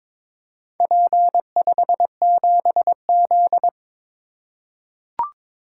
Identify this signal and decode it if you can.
Morse code, heard as a keyed tone, about 22 wpm, 700 Hz. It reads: P57Z
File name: practice / 22wpm / Four Letter-Number Combinations - 22wpm